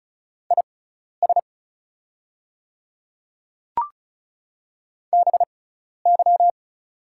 Morse code, heard as a keyed tone, about 36 wpm, spaced as for 12 wpm, 690 Hz. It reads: IS BY
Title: 6x Character Spacing - Top 100 Words - 35wpm